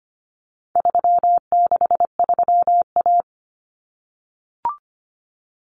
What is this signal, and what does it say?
Morse code, heard as a keyed tone, about 25 wpm, 700 Hz. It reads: 363A
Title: Common Cut Numbers - 4-Digit 25wpm